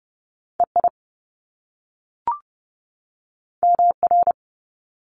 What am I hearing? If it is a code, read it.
Morse code, heard as a keyed tone, about 30 wpm, 700 Hz. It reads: EI MR
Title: Top Two Letter Combinations in Words 30wpm